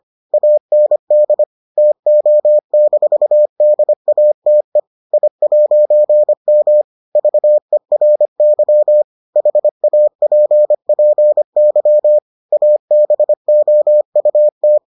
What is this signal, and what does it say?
Morse code, heard as a keyed tone, about 25 wpm, 600 Hz. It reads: AND TO-DATE I'M VERY HAPPY ABOUT